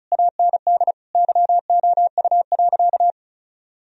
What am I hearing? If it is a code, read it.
Morse code, heard as a keyed tone, about 36 wpm, 700 Hz. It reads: AND YOU.
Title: Sentences from Top 1000 Words - Review Part 1 - 35wpm